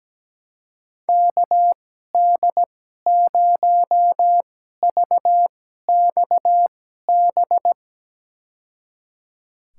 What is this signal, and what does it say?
Morse code, heard as a keyed tone, about 17 wpm, spaced as for 12 wpm, 700 Hz. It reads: KD0VXB
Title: USA Call Signs - 6 Characters - 2x Character Spacing - 17wpm